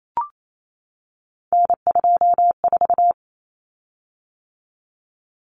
Morse code, heard as a keyed tone, about 28 wpm, 700 Hz. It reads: N24
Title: Common Cut Numbers - 3-Digit 28wpm